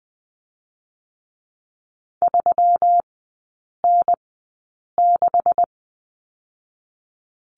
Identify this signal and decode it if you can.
Morse code, heard as a keyed tone, about 20 wpm, 700 Hz. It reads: 3 N 6
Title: 127 Lesson 21e - Number 6 - Three Characters 20wpm